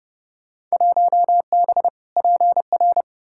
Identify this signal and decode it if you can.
Morse code, heard as a keyed tone, about 30 wpm, 700 Hz. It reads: 1B PR
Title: Field Day Exchanges Encore - 30wpm